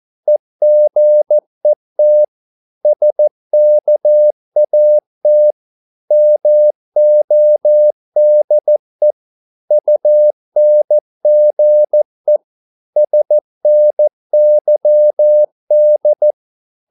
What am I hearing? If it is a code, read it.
Morse code, heard as a keyed tone, about 14 wpm, 600 Hz. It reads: EGET SKAT MODE UNGE SNYD
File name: Korte ord DK 14wpm